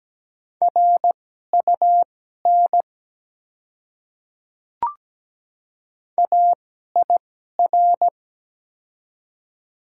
Morse code, heard as a keyed tone, about 17 wpm, spaced as for 12 wpm, 690 Hz. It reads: RUN AIR
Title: 2x Character Spacing - Top 300 Words - 17wpm